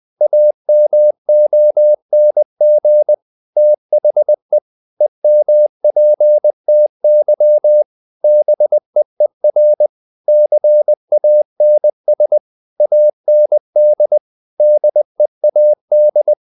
Morse code, heard as a keyed tone, about 20 wpm, 600 Hz. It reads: AMONG THE EMPTY BEER CANS AND DEAD
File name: the old man and the sea part 2 20wpm